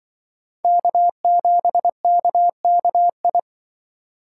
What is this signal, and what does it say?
Morse code, heard as a keyed tone, about 25 wpm, 700 Hz. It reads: K7KKI
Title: Speed Racing - USA 1x3 Call Signs 20wpm